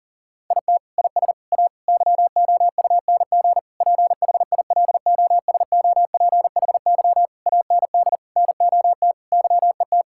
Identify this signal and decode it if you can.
Morse code, heard as a keyed tone, about 40 wpm, 700 Hz. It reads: IT IS A YOUNG PHILOSOPHY AND NOT YET
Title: Sentences from Top 1000 Words - Review Part 6 - 40wpm